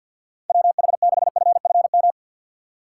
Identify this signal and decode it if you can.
Morse code, heard as a keyed tone, about 52 wpm, 700 Hz. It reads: WH6FFM
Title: USA Call Signs - 6x Character Spacing - 50wpm